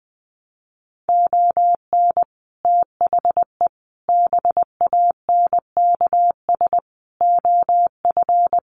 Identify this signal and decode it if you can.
Morse code, heard as a keyed tone, about 20 wpm, 700 Hz. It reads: ON THE BANKS OF
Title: Sets of 5 Words Encore2 20wpm